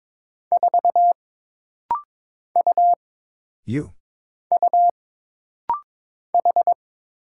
Morse code, heard as a keyed tone, about 22 wpm, 700 Hz. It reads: 4 U U H
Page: This rapid-fire timing is geared to push your ICR (Instant Character Recognition) skills.